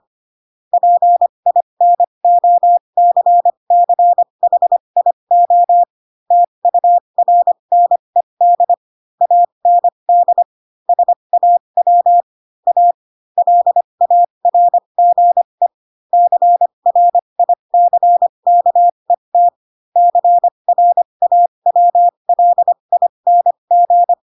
Morse code, heard as a keyed tone, about 25 wpm, spaced as for 22 wpm, 700 Hz. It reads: PINOCCHIO TURNED AND SAW A LARGE CRICKET CRAWLING
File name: Pinocchio / Pinocchio ch4 18 25 ik1hge